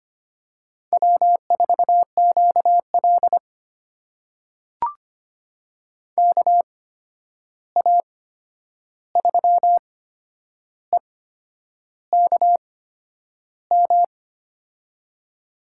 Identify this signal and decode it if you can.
Morse code, heard as a keyed tone, about 25 wpm, spaced as for 9 wpm, 700 Hz. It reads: W4QL KA3EKM
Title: USA Call Signs - 8x Character Spacing - 25wpm